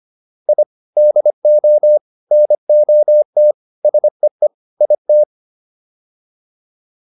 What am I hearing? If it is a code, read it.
Morse code, heard as a keyed tone, about 25 wpm, 600 Hz. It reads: I DO NOT SEE IT